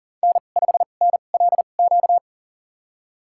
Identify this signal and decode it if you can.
Morse code, heard as a keyed tone, about 40 wpm, spaced as for 29 wpm, 700 Hz. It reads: N5NLQ
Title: USA Call Signs - 5 Characters - 2x Character Spacing - 40wpm